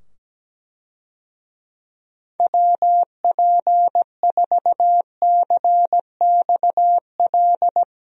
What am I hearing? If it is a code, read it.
Morse code, heard as a keyed tone, about 17 wpm, 700 Hz. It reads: WP4CXL
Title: USA Call Signs - 6 Characters - 17wpm